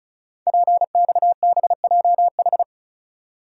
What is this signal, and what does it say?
Morse code, heard as a keyed tone, about 36 wpm, 690 Hz. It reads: PXBJH
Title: Code Groups - 5 Letters - 35wpm